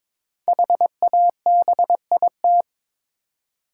Morse code, heard as a keyed tone, about 22 wpm, 700 Hz. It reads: HABIT
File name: 5 Letter Words - First 3 Letters Given - 22wpm